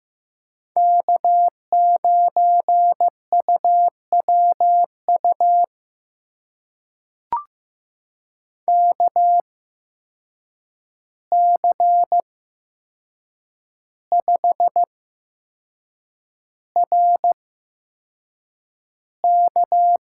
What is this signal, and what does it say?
Morse code, heard as a keyed tone, about 15 wpm, spaced as for 6 wpm, 700 Hz. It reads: K9UWU KC5RK
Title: USA Call Signs - 5 Characters - 8x Character Spacing - 15wpm